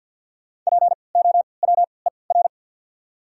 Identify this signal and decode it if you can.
Morse code, heard as a keyed tone, about 47 wpm, spaced as for 28 wpm, 700 Hz. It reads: POWER
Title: 3x Character Spacing - Top 500 Words - 50wpm